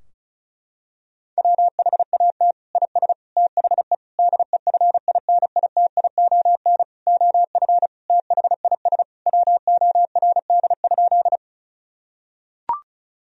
Morse code, heard as a keyed tone, about 36 wpm, 700 Hz. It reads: WHAT IS THE DEFINITION OF THIS WORD?
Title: Sentences from Top 1000 Words - Review Part 4 - 35wpm